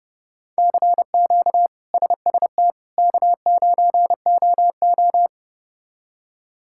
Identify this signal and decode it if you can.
Morse code, heard as a keyed tone, about 29 wpm, 700 Hz. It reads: CQ SST K9OO
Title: K1USN SST QSO Practice - 30wpm